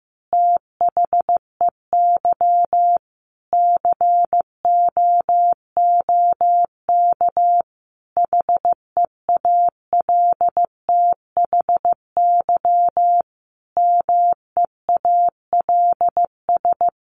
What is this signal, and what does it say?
Morse code, heard as a keyed tone, about 15 wpm, 700 Hz. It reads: THEY COOK HEALTHY MEALS
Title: Easy Phrases 15wpm